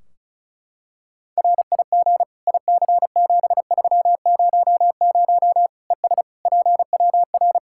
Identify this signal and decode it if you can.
Morse code, heard as a keyed tone, about 34 wpm, 700 Hz. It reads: RIG IC7300 ES PWR
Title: Practice QSO - Exchange 2 - Info - 35wpm